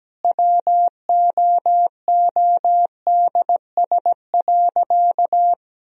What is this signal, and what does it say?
Morse code, heard as a keyed tone, about 17 wpm, 700 Hz. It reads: WOODS.
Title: Sentences from Top 1000 Words - Review Part 2 - 17wpm